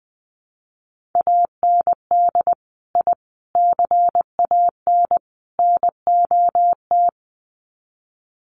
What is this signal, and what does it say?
Morse code, heard as a keyed tone, about 20 wpm, 700 Hz. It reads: AND I CAN NOT
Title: Sets of 4 Words 20wpm